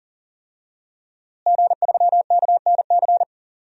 Morse code, heard as a keyed tone, about 40 wpm, 700 Hz. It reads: G3KNC
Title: DXCC-100 Call Signs - 5 Characters - Repeated 3x - 40wpm